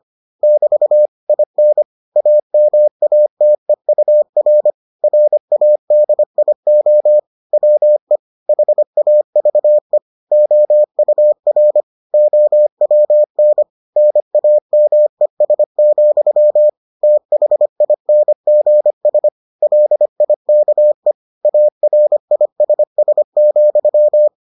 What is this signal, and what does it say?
Morse code, heard as a keyed tone, about 25 wpm, 600 Hz. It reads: <BT> IN AMATEUR RADIO WE HAVE OUR OWN NAMES, THINGS LIKE ARISS,